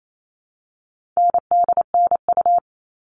Morse code, heard as a keyed tone, about 28 wpm, 700 Hz. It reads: NDNU
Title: All Cut Numbers - 4-Digit 28wpm